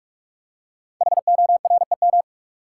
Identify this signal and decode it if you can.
Morse code, heard as a keyed tone, about 44 wpm, 700 Hz. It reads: HOREM
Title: Code Groups - 5 Letters - 45wpm